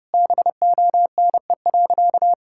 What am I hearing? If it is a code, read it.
Morse code, heard as a keyed tone, about 32 wpm, 700 Hz. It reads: BONE.
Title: Sentences from Top 1600 Words 30wpm